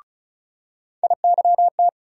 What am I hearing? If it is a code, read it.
Morse code, heard as a keyed tone, about 36 wpm, 700 Hz. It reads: IYT
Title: USA Call Sign Postfixes - 3 Letters 35wpm